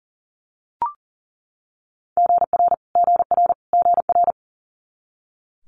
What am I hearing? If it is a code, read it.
Morse code, heard as a keyed tone, about 42 wpm, 700 Hz. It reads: GR GR GR
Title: Multi-letter Phonemes Repeated 3x 40wpm